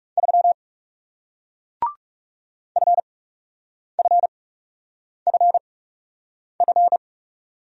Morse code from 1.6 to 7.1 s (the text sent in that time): FFFF